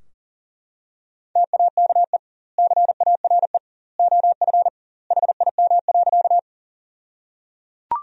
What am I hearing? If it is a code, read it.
Morse code, heard as a keyed tone, about 37 wpm, 700 Hz. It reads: TAKE CARE OF HIM.
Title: 2x Word-Spacing - Sentences from Top 300 Words - 40wpm